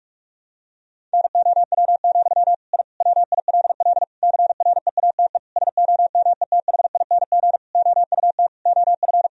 Morse code, heard as a keyed tone, about 46 wpm, 700 Hz. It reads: NOW, I WILL CREATE SOMETHING OUT OF
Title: Sentences from Top 1000 Words - Review Part 3 - 45wpm